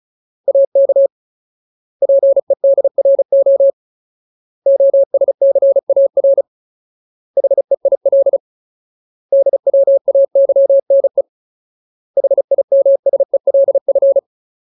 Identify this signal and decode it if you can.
Morse code, heard as a keyed tone, about 34 wpm, 550 Hz. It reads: AK PEDRO OSCAR HEIL DWAYNE HIMSELF
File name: Ultimate Word List / 35WPM Ultimate